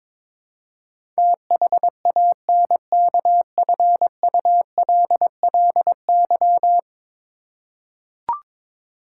Morse code, heard as a keyed tone, about 22 wpm, 700 Hz. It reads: THANKFULLY
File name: practice / 22wpm / 10 Letter Words - First 3 Letters Given - 22wpm